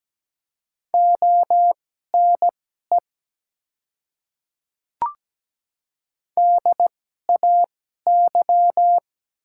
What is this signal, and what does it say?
Morse code, heard as a keyed tone, about 17 wpm, spaced as for 12 wpm, 700 Hz. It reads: ONE DAY
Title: 2x Character Spacing - Top 100 Words - 17wpm